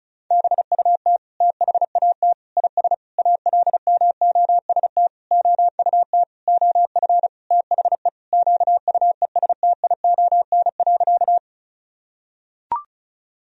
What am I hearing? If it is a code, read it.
Morse code, heard as a keyed tone, about 36 wpm, 700 Hz. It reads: BUT THAT IS ALMOST OUT OF THE QUESTION.
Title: Sentences from Top 1000 Words - Review Part 4 - 35wpm